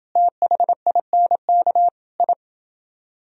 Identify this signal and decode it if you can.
Morse code, heard as a keyed tone, about 29 wpm, 700 Hz. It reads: THINK I